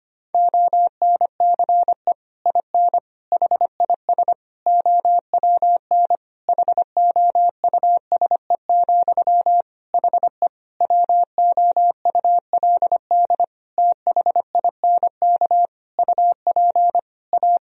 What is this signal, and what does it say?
Morse code, heard as a keyed tone, about 25 wpm, 700 Hz. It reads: ONCE IN HIS OWN HOUSE, HE WOULD THINK UP A
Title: Sentences from Top 1000 Words - Review Part 2 - 25wpm